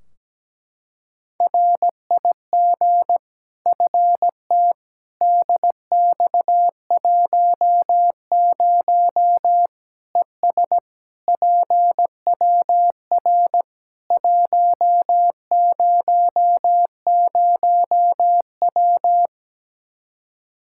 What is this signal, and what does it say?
Morse code, heard as a keyed tone, about 17 wpm, 700 Hz. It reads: RIG FT DX10 ES PWR 100W
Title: K1USN SST QSO Practice 17wpm